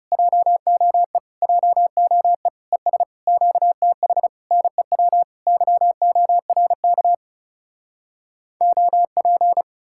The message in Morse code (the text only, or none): JOE JOE ES QTH NEW YORK OP